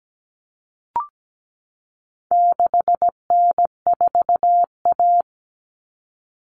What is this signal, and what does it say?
Morse code, heard as a keyed tone, about 17 wpm, 700 Hz. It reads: 6N4A
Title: Common Cut Numbers - 4-Digit 17wpm